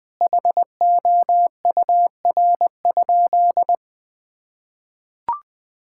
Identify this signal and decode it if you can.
Morse code, heard as a keyed tone, about 21 wpm, 700 Hz. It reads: HOUR?
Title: Sentences from Top 1000 Words - Review Part 5 - 20wpm